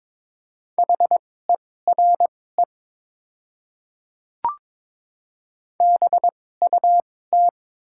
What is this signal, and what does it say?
Morse code, heard as a keyed tone, about 22 wpm, spaced as for 16 wpm, 700 Hz. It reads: HERE BUT
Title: 2x Character Spacing - Top 300 Words - 22wpm